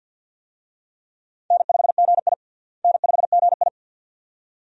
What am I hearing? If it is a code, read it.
Morse code, heard as a keyed tone, about 53 wpm, 700 Hz. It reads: N5GI N5GI
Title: USA Call Signs - 4 Characters - Repeated 2x - 50wpm